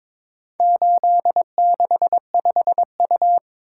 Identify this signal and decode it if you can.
Morse code, heard as a keyed tone, about 21 wpm, 700 Hz. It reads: 865U
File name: practice / 22wpm / Four Letter-Number Combinations - 22wpm